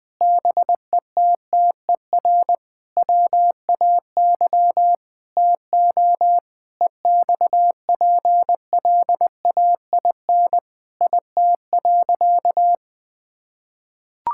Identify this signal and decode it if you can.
Morse code, heard as a keyed tone, about 20 wpm, 700 Hz. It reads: BETTER WAY TO EXPLAIN IT.
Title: Sentences with 8-Word Phrases - Spoken Beginning - 20wpm